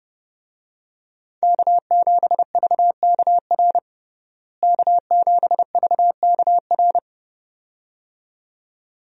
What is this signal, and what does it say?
Morse code, heard as a keyed tone, about 32 wpm, 700 Hz. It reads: K7VKR K7VKR
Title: USA Call Signs - 5 Characters - Repeated 2x - 30wpm